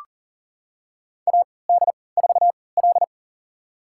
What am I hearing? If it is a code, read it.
Morse code, heard as a keyed tone, about 40 wpm, spaced as for 23 wpm, 700 Hz. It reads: AD4L
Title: USA Call Signs - 4 Characters - 3x Character Spacing - 40wpm